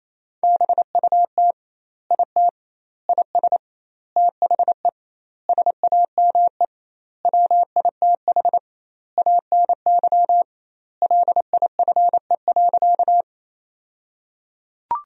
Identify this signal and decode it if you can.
Morse code, heard as a keyed tone, about 28 wpm, 700 Hz. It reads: BUT IT IS THE SAME WITH ANY LIFE.
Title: 2x Word-Spacing - Sentences from Top 300 Words - 28wpm